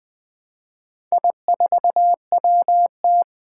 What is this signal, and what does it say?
Morse code, heard as a keyed tone, about 20 wpm, 700 Hz. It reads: I4WT
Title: Four Challenging Letter-Number Combinations 20wpm